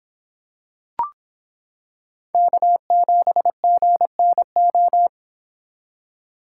Morse code, heard as a keyed tone, about 26 wpm, 700 Hz. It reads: K7GNO